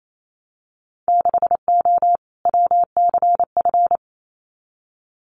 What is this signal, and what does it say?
Morse code, heard as a keyed tone, about 28 wpm, 700 Hz. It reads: 6O WCF
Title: Winter Field Day Exchanges Encore 28wpm